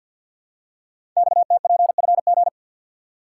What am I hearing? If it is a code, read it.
Morse code, heard as a keyed tone, about 50 wpm, 700 Hz. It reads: XTPFG
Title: Code Groups - 5 Letters - 50wpm